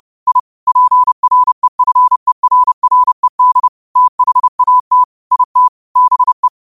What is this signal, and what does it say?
Morse code, heard as a keyed tone, about 30 wpm, 1000 Hz. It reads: I PREFERRED THAT IT BE
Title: 30 WPM morse code quotes for Fri, 01 Mar 2024